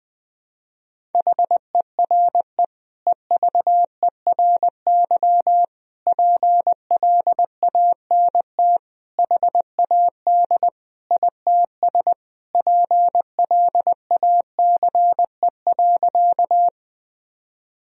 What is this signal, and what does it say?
Morse code, heard as a keyed tone, about 21 wpm, 700 Hz. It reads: HERE EVERY PLANT HAD ITS PLACE.
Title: Sentences from Top 1000 Words - Review Part 1 - 20wpm